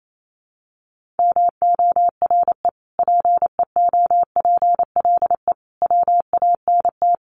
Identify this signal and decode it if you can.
Morse code, heard as a keyed tone, about 28 wpm, 700 Hz. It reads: MORE PEOPLE WANT
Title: Sentences from Top 1500 Words 28wpm